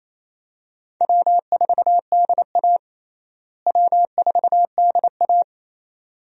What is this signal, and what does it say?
Morse code, heard as a keyed tone, about 28 wpm, 700 Hz. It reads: W4DA W4DA
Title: USA Call Signs - 4 Characters - Repeated 2x - 28wpm